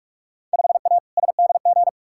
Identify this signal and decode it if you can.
Morse code, heard as a keyed tone, about 47 wpm, 700 Hz. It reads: 5A SDG
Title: Field Day Exchanges Encore - 45wpm